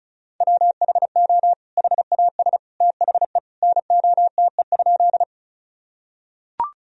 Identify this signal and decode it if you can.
Morse code, heard as a keyed tone, about 34 wpm, 700 Hz. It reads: WHO HAS THE NOTE?
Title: Sentences from Top 1000 Words - Review Part 2 - 35wpm